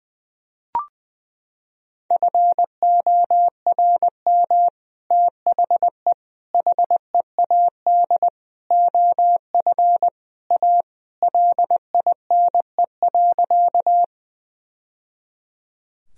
Morse code, heard as a keyed tone, about 20 wpm, 700 Hz. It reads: FORM THE HEAD OF A LINE.
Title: Sentences from Top 1000 Words - Review Part 1 - 20wpm